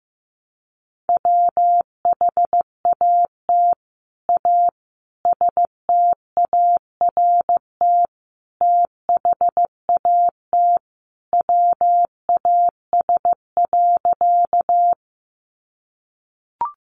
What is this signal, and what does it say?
Morse code, heard as a keyed tone, about 15 wpm, 700 Hz. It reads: WHAT A START THAT WAS.
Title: Sentences from Top 300 Words 15wpm